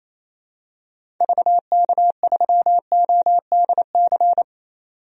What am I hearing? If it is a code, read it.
Morse code, heard as a keyed tone, about 28 wpm, 700 Hz. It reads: VK3ODC
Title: DXCC-100 Call Signs - 6 Characters - 28wpm